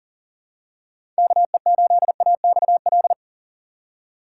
Morse code, heard as a keyed tone, about 40 wpm, 700 Hz. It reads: KE8AXL
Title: USA Call Signs - 6 Characters - 4x Character Spacing - 40wpm